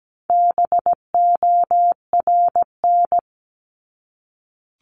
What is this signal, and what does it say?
Morse code, heard as a keyed tone, about 17 wpm, 700 Hz. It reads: BORN
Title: Top 751-1000 Words 17wpm